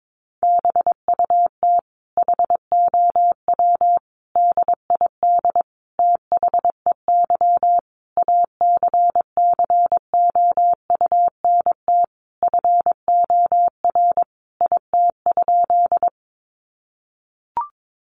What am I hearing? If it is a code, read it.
Morse code, heard as a keyed tone, about 22 wpm, 700 Hz. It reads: BUT HOW DID THEY ACCOUNT FOR IT?
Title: Sentences from Top 900 Words 22wpm